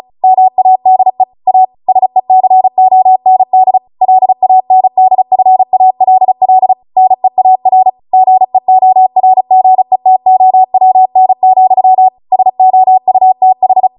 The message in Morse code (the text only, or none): MADE A SECOND LANDFALL NEAR GEORGETOWN, SOUTH